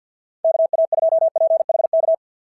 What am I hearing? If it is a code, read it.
Morse code, heard as a keyed tone, about 51 wpm, 690 Hz. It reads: KA2PHK
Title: SOTA QSO Practice 50wpm